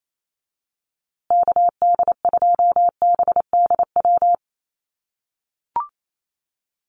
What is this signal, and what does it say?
Morse code, heard as a keyed tone, about 28 wpm, 700 Hz. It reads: KD2BDW
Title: USA Call Signs Repeated 3x 28wpm